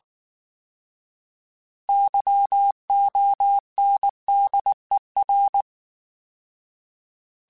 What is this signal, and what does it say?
Morse code, heard as a keyed tone, about 19 wpm, 800 Hz. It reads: YONDER
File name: Alphabetical / Alphabetical mp3s / Ys @18